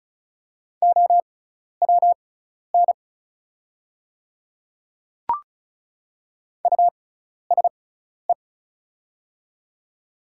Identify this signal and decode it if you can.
Morse code, heard as a keyed tone, about 36 wpm, spaced as for 12 wpm, 700 Hz. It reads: OWN USE
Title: 6x Character Spacing - Top 300 Words - 35wpm